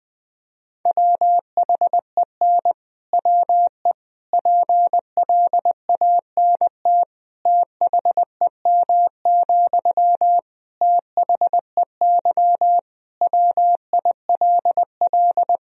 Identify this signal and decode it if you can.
Morse code, heard as a keyed tone, about 21 wpm, 700 Hz. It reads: WHEN WE PLANT THEM, THEY WILL
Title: Sentences from Top 1000 Words - Review Part 1 - 20wpm